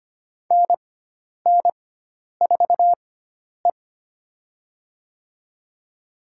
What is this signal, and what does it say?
Morse code, heard as a keyed tone, about 25 wpm, spaced as for 10 wpm, 700 Hz. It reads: NN4E
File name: Lesson 16g - Letter c - Call Signs 25wpm